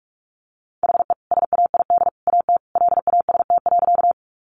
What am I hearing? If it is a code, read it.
Morse code, heard as a keyed tone, about 44 wpm, 700 Hz. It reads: HE SAID AT LAST.
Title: Sentences from Top 300 Words 45wpm